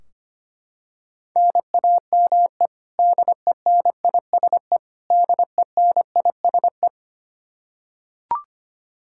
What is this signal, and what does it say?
Morse code, heard as a keyed tone, about 25 wpm, 700 Hz. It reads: NAME DENISE DENISE
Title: Practice QSO - Exchange 1 - Beginning - 25wpm